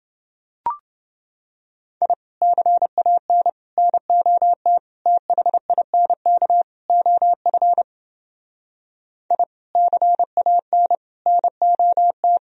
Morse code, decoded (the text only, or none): I CAN NOT THINK OF I CAN NOT